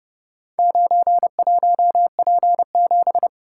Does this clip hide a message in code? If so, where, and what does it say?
Yes: Morse code from 0.6 to 3.3 s, heard as a keyed tone, about 30 wpm, 700 Hz. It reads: 91P7